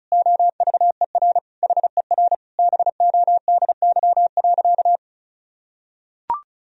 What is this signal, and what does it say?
Morse code, heard as a keyed tone, about 36 wpm, 700 Hz. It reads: OVER HER BODY.
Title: Sentences from Top 1000 Words - Review Part 2 - 35wpm